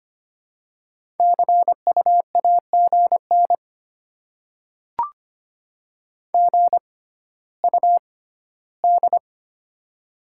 Morse code, heard as a keyed tone, about 25 wpm, spaced as for 14 wpm, 700 Hz. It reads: CU AGN GUD
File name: Top 20 QSO Elements - 6x Character Spacing - 25wpm